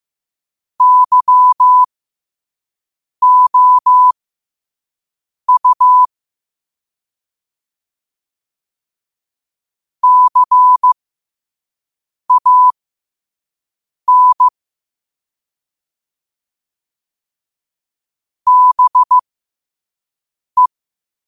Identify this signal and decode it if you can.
Morse code, heard as a keyed tone, about 15 wpm, spaced as for 5 wpm, 1000 Hz. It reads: YOU CAN BE
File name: qotdfiles / 5wpm